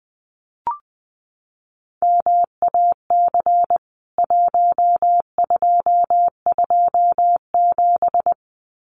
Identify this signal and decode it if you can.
Morse code, heard as a keyed tone, about 20 wpm, 700 Hz. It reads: MAC 1227
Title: CWT Exchange 20wpm